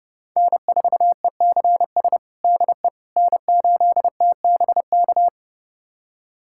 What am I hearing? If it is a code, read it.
Morse code, heard as a keyed tone, about 32 wpm, 700 Hz. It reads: N4ECS DE N8TBK
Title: USA Callsign DE Callsign - 5 Characters - 30wpm